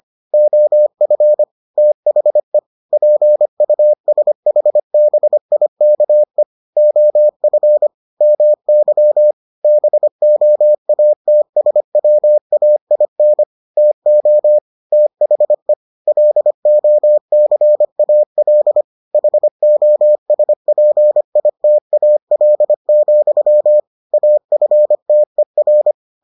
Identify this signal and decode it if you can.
Morse code, heard as a keyed tone, about 25 wpm, 600 Hz. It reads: OF THE PUSHBIKE OF MY BOATSWAIN TO THE LOCAL HOSPITAL, AFTER